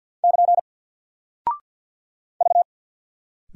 Morse code, heard as a keyed tone, about 50 wpm, 700 Hz. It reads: <KN> V
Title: Single Character (Letter:Number:Punctuation:Prosign) Alt 50wpm